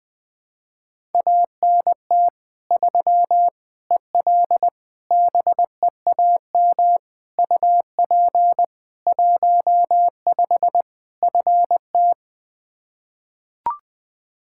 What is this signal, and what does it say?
Morse code, heard as a keyed tone, about 21 wpm, 700 Hz. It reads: ANT 3 EL BEAM UP 15 FT
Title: Practice QSO - Exchange 2 - Info - 20wpm